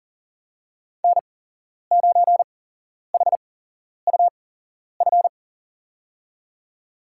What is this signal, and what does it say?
Morse code, heard as a keyed tone, about 37 wpm, spaced as for 11 wpm, 700 Hz. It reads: N9HUF
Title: USA Call Signs - 8x Character Spacing - 40wpm